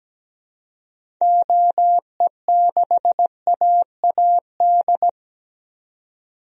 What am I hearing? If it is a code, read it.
Morse code, heard as a keyed tone, about 17 wpm, 700 Hz. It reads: OE6AAD
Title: DXCC-100 Call Signs - 6 Characters - 17wpm